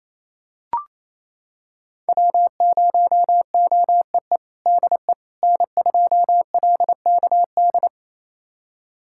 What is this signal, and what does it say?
Morse code, heard as a keyed tone, about 28 wpm, 700 Hz. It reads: W0OEE DE N2LKD
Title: USA Callsign DE Callsign - 5 Characters - 28wpm